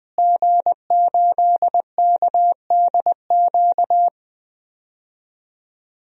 Morse code, heard as a keyed tone, about 20 wpm, 700 Hz. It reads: G8KDQ
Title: DXCC-100 Call Signs - 5 Characters - Repeated 2x - 20wpm